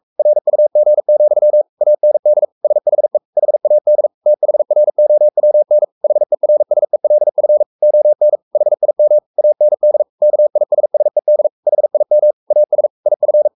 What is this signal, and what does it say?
Morse code, heard as a keyed tone, about 44 wpm, 600 Hz. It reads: RUG, AND SHE HAD THROWN HERSELF ON HIM AND KISSED HIM AS IF
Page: Romanen Never af Ken Follet på engelsk.